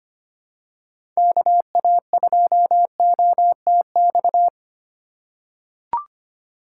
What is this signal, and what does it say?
Morse code, heard as a keyed tone, about 25 wpm, 700 Hz. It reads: KA2OTX
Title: USA Call Signs - 6 Characters - 4x Character Spacing - 25wpm